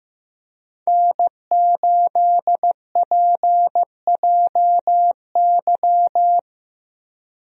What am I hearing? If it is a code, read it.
Morse code, heard as a keyed tone, about 15 wpm, 700 Hz. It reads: N8PJY
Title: USA Call Signs - 5 Characters - Repeated 3x - 15wpm